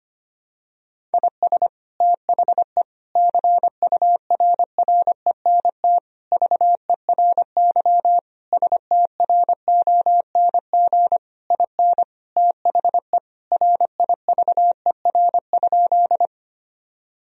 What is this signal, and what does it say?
Morse code, heard as a keyed tone, about 25 wpm, 700 Hz. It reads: IS THE CURRENT VERY STRONG IN THE RIVER?
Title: Sentences from Top 1000 Words - Review Part 4 - 25wpm